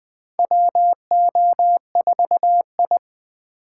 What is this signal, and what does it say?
Morse code, heard as a keyed tone, about 20 wpm, 700 Hz. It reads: WO4I
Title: USA Call Signs - 4 Characters - 3x Character Spacing - 20wpm